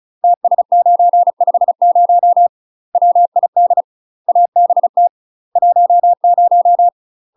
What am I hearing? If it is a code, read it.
Morse code, heard as a keyed tone, about 36 wpm, 700 Hz. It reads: TS950 WID ABT 10
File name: UK MASTER QSOs TRAINING / 35WPM / UK MASTER QSOs 0002